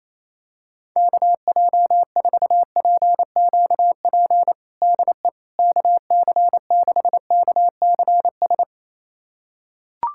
Morse code, heard as a keyed tone, about 28 wpm, 700 Hz. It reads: KJ4PQP DE KC6KCS
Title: USA Callsign DE Callsign - 6 Characters - 28wpm